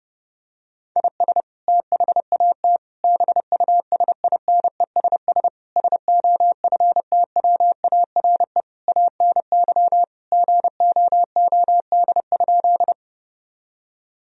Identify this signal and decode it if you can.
Morse code, heard as a keyed tone, about 30 wpm, 700 Hz. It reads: IS THAT BUSINESS SOFTWARE ANY GOOD?
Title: Sentences from Top 1000 Words - Review Part 3 - 30wpm